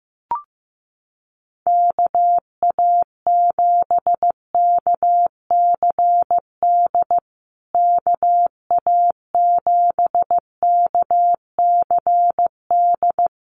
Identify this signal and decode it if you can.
Morse code, heard as a keyed tone, about 15 wpm, 700 Hz. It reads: KA7KCD KA7KCD
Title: USA Call Signs Repeated 3x 15wpm